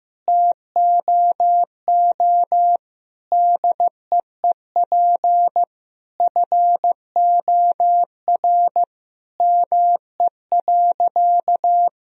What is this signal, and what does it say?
Morse code, heard as a keyed tone, about 15 wpm, 700 Hz. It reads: TOO DEEP FOR ME.
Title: Sentences from Top 1000 Words - Review Part 3 - 15wpm